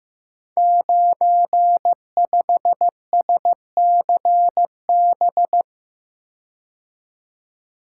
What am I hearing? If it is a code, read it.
Morse code, heard as a keyed tone, about 15 wpm, 700 Hz. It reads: 95SCB
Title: Code Groups - 5 Letter:Numbers - 15wpm